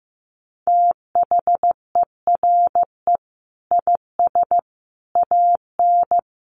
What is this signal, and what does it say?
Morse code, heard as a keyed tone, about 14 wpm, 700 Hz. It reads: THERE IS AN
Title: Sentences from Top 1400 Words 15wpm